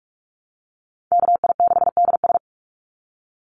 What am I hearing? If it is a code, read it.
Morse code, heard as a keyed tone, about 46 wpm, 700 Hz. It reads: KI6DS
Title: NA SOTA Chaser Callsigns 45wpm